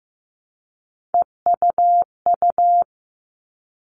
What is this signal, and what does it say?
Morse code, heard as a keyed tone, about 15 wpm, 700 Hz. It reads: EUU
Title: All Cut Numbers - 3-Digit 15wpm